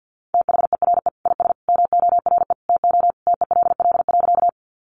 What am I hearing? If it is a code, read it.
Morse code, heard as a keyed tone, about 50 wpm, 700 Hz. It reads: THERE IS MORE TO TELL.
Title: Sentences from Top 200 Words 50wpm